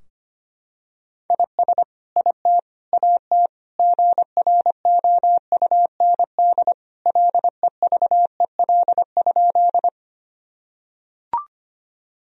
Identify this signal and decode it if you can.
Morse code, heard as a keyed tone, about 25 wpm, 700 Hz. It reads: IS IT AT GROUND LEVEL?
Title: Sentences from Top 1000 Words - Review Part 4 - 25wpm